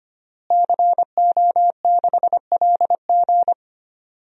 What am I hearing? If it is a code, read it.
Morse code, heard as a keyed tone, about 25 wpm, 700 Hz. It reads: CO6LG
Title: DXCC-100 Call Signs - 5 Characters - Repeated 3x - 25wpm